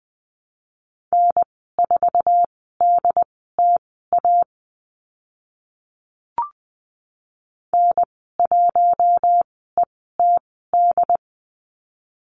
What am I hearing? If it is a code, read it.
Morse code, heard as a keyed tone, about 20 wpm, spaced as for 14 wpm, 700 Hz. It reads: N4DTA N1ETD
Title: 069 Lesson 12g - Letter d - Call Signs 20wpm